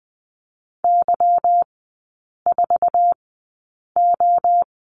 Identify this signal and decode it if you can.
Morse code, heard as a keyed tone, about 20 wpm, spaced as for 8 wpm, 700 Hz. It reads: Y4O
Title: 145 Lesson 24e - Letter y - Three Characters 20wpm